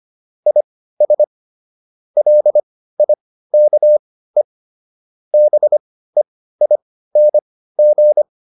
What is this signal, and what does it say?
Morse code, heard as a keyed tone, about 25 wpm, spaced as for 15 wpm, 600 Hz. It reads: IS LIKE BEING